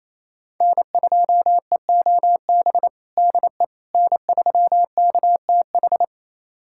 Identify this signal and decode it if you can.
Morse code, heard as a keyed tone, about 28 wpm, 700 Hz. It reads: N2EOB DE N3KTH
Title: USA Callsign DE Callsign - 5 Characters - 28wpm